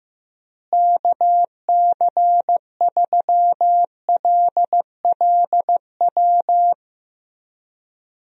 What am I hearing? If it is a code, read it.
Morse code, heard as a keyed tone, about 15 wpm, 700 Hz. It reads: KC3LLW
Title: USA Call Signs - 6 Characters - Repeated 3x - 15wpm